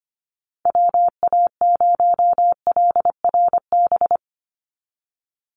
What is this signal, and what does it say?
Morse code, heard as a keyed tone, about 25 wpm, 700 Hz. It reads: WA0LRB
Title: USA Call Signs 25wpm